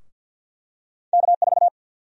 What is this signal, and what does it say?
Morse code, heard as a keyed tone, about 49 wpm, 700 Hz. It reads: K4
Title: USA Call Sign Prefixes - Repeated 2x - 50wpm